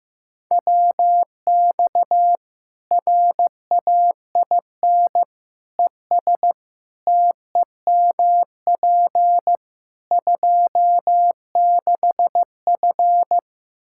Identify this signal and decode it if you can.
Morse code, heard as a keyed tone, about 15 wpm, 700 Hz. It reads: WX RAIN ES TEMP 26F
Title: Practice QSO - Exchange 2 - Info - 15wpm